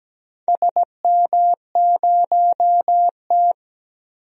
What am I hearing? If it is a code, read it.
Morse code, heard as a keyed tone, about 17 wpm, 700 Hz. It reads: SM0T
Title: DXCC-100 Call Signs - 4 Characters - 17wpm